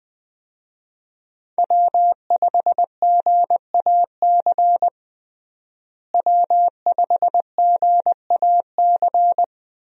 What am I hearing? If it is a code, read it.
Morse code, heard as a keyed tone, about 21 wpm, 700 Hz. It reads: W5GAC W5GAC
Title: USA Call Signs - 5 Characters - Repeated 2x - 20wpm